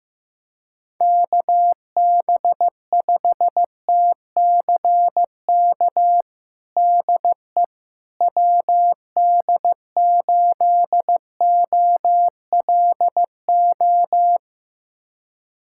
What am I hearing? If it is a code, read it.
Morse code, heard as a keyed tone, about 15 wpm, 700 Hz. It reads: KB5TCK DE WD8OLO
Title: USA Callsign DE Callsign - 6 Characters - 15wpm